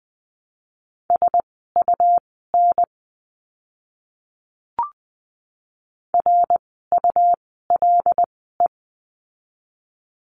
Morse code, heard as a keyed tone, about 20 wpm, spaced as for 15 wpm, 700 Hz. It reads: SUN RULE
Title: 101 Lesson 17f - Letter u - Words 20wpm